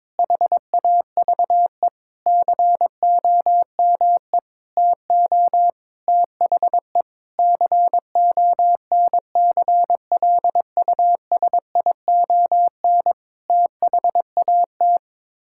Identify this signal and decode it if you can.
Morse code, heard as a keyed tone, about 21 wpm, 700 Hz. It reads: HAVE COME TO THE CONCLUSION THAT
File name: practice / 22wpm / Sets of 7 Words - 22wpm